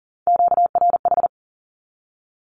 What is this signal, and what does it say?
Morse code, heard as a keyed tone, about 39 wpm, 700 Hz. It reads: QRH
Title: Q-Codes Repeated 3x 40wpm